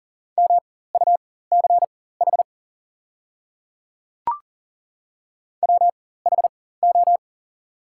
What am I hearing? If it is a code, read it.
Morse code, heard as a keyed tone, about 41 wpm, spaced as for 19 wpm, 700 Hz. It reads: MUCH WHO
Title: 4x Character Spacing - Top 300 Words - 40wpm